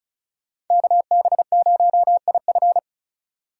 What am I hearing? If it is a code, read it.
Morse code, heard as a keyed tone, about 43 wpm, 700 Hz. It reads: KB0IF
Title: Speed Racing - USA 2x2 Call Signs 35wpm